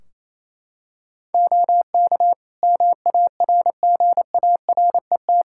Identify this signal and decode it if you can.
Morse code, heard as a keyed tone, about 27 wpm, 700 Hz. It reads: OK MARGARET
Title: Practice QSO - Exchange 4 - Ending - 28wpm